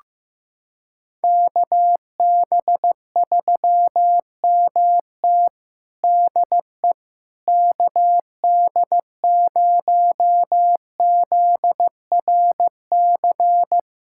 Morse code, heard as a keyed tone, about 15 wpm, 700 Hz. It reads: KB3MT DE KD0ZRC
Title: Practice QSO - Exchange 4 - Ending - 15wpm